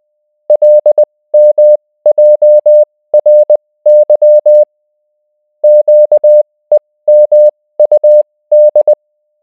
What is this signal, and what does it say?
Morse code, heard as a keyed tone, about 20 wpm, spaced as for 16 wpm, 600 Hz. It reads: LMJRY QEMUD